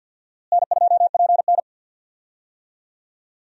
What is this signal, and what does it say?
Morse code, heard as a keyed tone, about 45 wpm, 700 Hz. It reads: N2PN
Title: USA Call Signs - 4 Characters - Repeated 3x - 50wpm